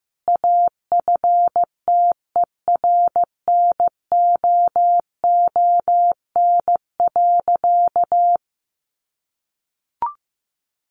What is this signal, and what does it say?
Morse code, heard as a keyed tone, about 15 wpm, 700 Hz. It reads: AFTERNOON.
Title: Sentences from Top 1500 Words 15wpm